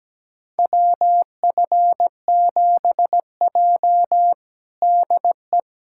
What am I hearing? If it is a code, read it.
Morse code, heard as a keyed tone, about 17 wpm, 700 Hz. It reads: WF7J DE
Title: USA Callsign DE Callsign - 4 Characters - 17wpm